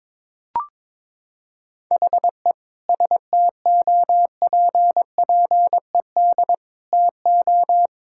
Morse code, heard as a keyed tone, about 22 wpm, 700 Hz. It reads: HE STOPPED TO
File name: practice / 22wpm / Sentences with 2-Word Phrases - 22wpm